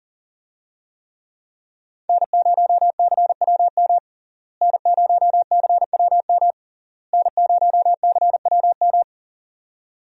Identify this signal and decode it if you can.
Morse code, heard as a keyed tone, about 40 wpm, 700 Hz. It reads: N0CWM N0CWM N0CWM
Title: USA Call Signs - 5 Characters - Repeated 3x - 40wpm